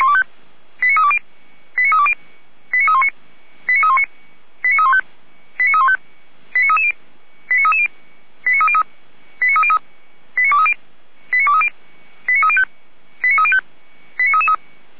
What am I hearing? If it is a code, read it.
Five-tone selective call (ZVEI), heard as quick runs of five tones, tones of about 70 ms. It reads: 89320 89320 89319 89319 89315 89315 89330 89330 89303 89303 89230 89230 89305 89305 89333